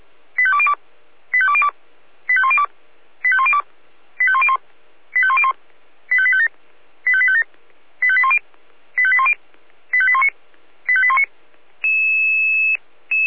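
Five-tone selective call (ZVEI), heard as quick runs of five tones, tones of about 70 ms. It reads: 85202 85202 85102 85102 85101 85101 85857 85857 85810 85810 85819 85819
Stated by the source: Five-tone ZVEI selective call received on an SDR.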